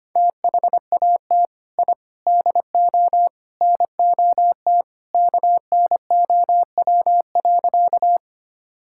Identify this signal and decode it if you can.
Morse code, heard as a keyed tone, about 25 wpm, 700 Hz. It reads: THAT I DO NOT KNOW.
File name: practice / 25wpm / Sentences with 4-Word Phrases - 25wpm